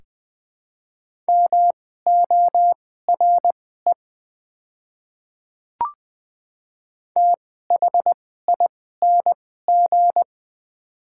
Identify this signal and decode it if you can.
Morse code, heard as a keyed tone, about 20 wpm, spaced as for 15 wpm, 700 Hz. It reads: MORE THING
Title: 2x Character Spacing - Top 200 Words - 20wpm